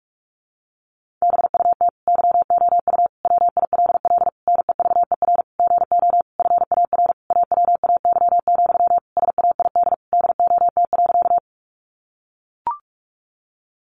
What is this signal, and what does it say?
Morse code, heard as a keyed tone, about 44 wpm, 700 Hz. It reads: BUT YOU WILL NEVER GO FAR AWAY, SAID DOT.
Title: Sentences from Top 1500 Words 45wpm